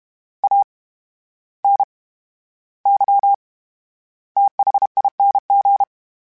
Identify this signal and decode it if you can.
Morse code, heard as a keyed tone, about 32 wpm, spaced as for 14 wpm, 800 Hz. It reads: ANYTHING
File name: Special INGs / Special INGs-mp3s / VerySlowToFaster Ing @30